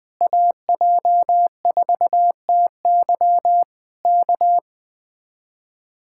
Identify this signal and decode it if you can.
Morse code, heard as a keyed tone, about 20 wpm, 700 Hz. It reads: AJ4TY K